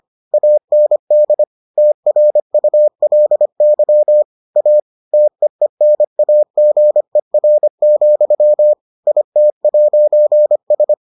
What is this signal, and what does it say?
Morse code, heard as a keyed tone, about 25 wpm, 600 Hz. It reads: AND TRULY A TEENAGER, IT'S